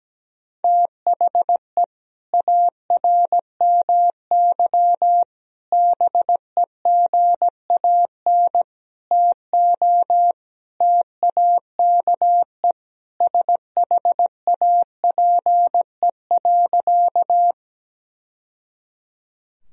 Morse code, heard as a keyed tone, about 17 wpm, 700 Hz. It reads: THE ARMY BEGAN TO TAKE SHAPE.
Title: Sentences from Top 1000 Words - Review Part 3 - 17wpm